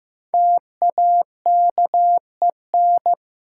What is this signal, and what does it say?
Morse code, heard as a keyed tone, about 15 wpm, 700 Hz. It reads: TAKEN
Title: Sentences from Top 1000 Words - Review Part 4 - New Word Drill - 15wpm